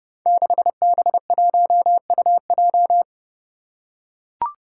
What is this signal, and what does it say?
Morse code, heard as a keyed tone, about 28 wpm, 700 Hz. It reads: 6B1UJ
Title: Five Challenging Letter-Number Combinations 30wpm